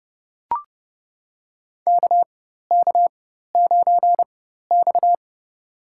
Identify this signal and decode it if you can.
Morse code, heard as a keyed tone, about 30 wpm, spaced as for 14 wpm, 700 Hz. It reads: KK9X
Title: USA Call Signs - 4 Characters - 4x Character Spacing - 30wpm